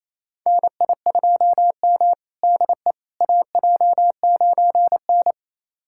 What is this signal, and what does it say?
Morse code, heard as a keyed tone, about 27 wpm, 700 Hz. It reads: NI2M DE AJ9N
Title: USA Callsign DE Callsign - 4 Characters - 28wpm